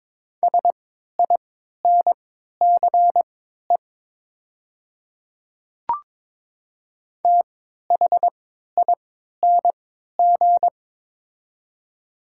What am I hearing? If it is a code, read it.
Morse code, heard as a keyed tone, about 22 wpm, spaced as for 12 wpm, 700 Hz. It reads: SINCE THING
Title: 3x Character Spacing - Top 300 Words - 22wpm